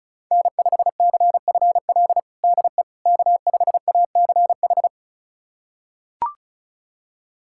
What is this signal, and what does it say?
Morse code, heard as a keyed tone, about 36 wpm, 700 Hz. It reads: N5CFL DE K5ACH
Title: USA Callsign DE Callsign - 5 Characters - 35wpm